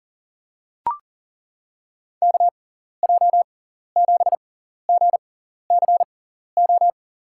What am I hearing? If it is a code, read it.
Morse code, heard as a keyed tone, about 40 wpm, spaced as for 14 wpm, 700 Hz. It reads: KJ7GCO
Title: USA Call Signs - 6x Character Spacing - 40wpm